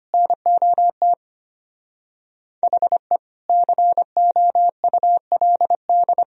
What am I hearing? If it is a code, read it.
Morse code, heard as a keyed tone, about 24 wpm, 700 Hz. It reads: NOT HE COULD
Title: Speed-Racing - Sets of 3 Words 25wpm